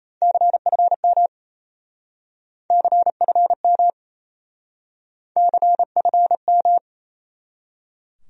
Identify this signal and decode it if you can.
Morse code, heard as a keyed tone, about 36 wpm, 700 Hz. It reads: CFM CFM CFM